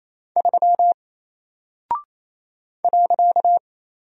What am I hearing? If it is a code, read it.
Morse code, heard as a keyed tone, about 27 wpm, 700 Hz. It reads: 3 .
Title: Single Character (Letter:Number:Punctuation:Prosign) Alt 28wpm